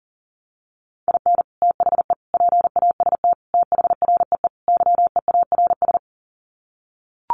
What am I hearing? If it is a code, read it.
Morse code, heard as a keyed tone, about 40 wpm, 700 Hz. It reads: IN THE PAST THREE YEARS
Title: Sets of 5 Words Encore2 40wpm